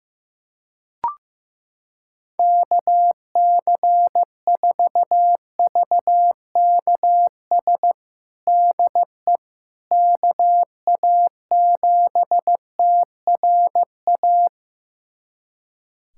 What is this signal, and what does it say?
Morse code, heard as a keyed tone, about 15 wpm, 700 Hz. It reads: KC4VKS DE KA7TRA
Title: USA Callsign DE Callsign 15wpm